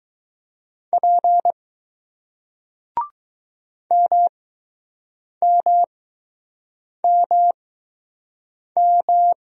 Morse code, heard as a keyed tone, about 20 wpm, spaced as for 6 wpm, 700 Hz. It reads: P MMMM